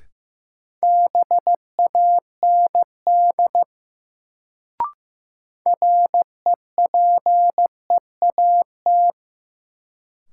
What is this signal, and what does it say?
Morse code, heard as a keyed tone, about 15 wpm, 700 Hz. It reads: BAND REPEAT
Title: Sentences from Top 1600 Words - New Word Drill Alt - 15wpm